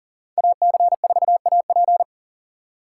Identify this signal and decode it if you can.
Morse code, heard as a keyed tone, about 37 wpm, 700 Hz. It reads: AC4AP
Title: USA Call Signs - 2x Character Spacing - 40wpm